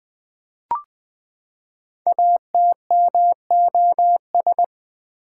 Morse code, heard as a keyed tone, about 20 wpm, 700 Hz. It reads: ATMOS
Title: Build A Word - 5 Letters At A Time - 10-Letter Words - 20wpm